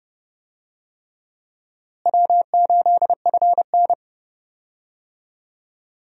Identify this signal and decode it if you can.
Morse code, heard as a keyed tone, about 30 wpm, 700 Hz. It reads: W8FN
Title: USA Call Signs - 4 Characters - Repeated 3x - 30wpm